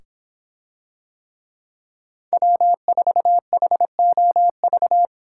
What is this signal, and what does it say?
Morse code, heard as a keyed tone, about 26 wpm, 700 Hz. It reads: W4HOV